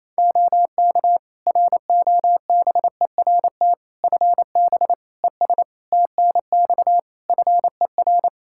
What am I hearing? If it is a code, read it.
Morse code, heard as a keyed tone, about 28 wpm, 700 Hz. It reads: OK ROBERT FB ES TNX FER
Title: K1USN SST QSO Practice 28wpm